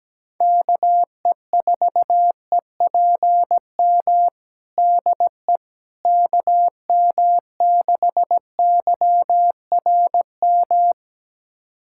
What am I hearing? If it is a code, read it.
Morse code, heard as a keyed tone, about 17 wpm, 700 Hz. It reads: KE4EPM DE KM6YRM
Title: USA Callsign DE Callsign - 6 Characters - 17wpm